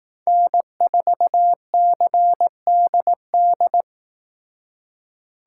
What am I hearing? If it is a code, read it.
Morse code, heard as a keyed tone, about 18 wpm, 700 Hz. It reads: N4CDD
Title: Speed Racing - USA 1x3 Call Signs 15wpm